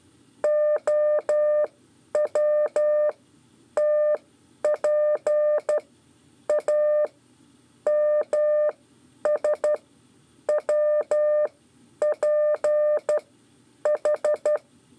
Morse code, heard as a keyed tone, about 12 wpm, spaced as for 8 wpm, 600 Hz. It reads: OWTPAMSWPH
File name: 5 LEZIONE RIPASSO 1